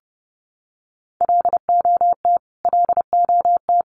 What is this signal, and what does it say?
Morse code, heard as a keyed tone, about 30 wpm, 700 Hz. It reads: LOT LOT
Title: Top 500 Words Repeated 30wpm